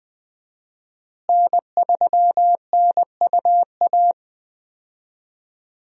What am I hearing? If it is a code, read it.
Morse code, heard as a keyed tone, about 21 wpm, 700 Hz. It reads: N3NUA
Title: USA Call Signs - 5 Characters - Repeated 3x - 20wpm